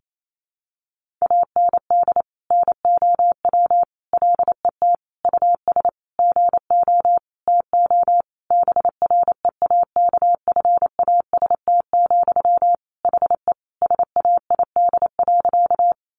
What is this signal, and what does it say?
Morse code, heard as a keyed tone, about 28 wpm, 700 Hz. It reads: AND NOW LET US GO TO BREAKFAST, HE SAID.
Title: Sentences from Top 1500 Words 28wpm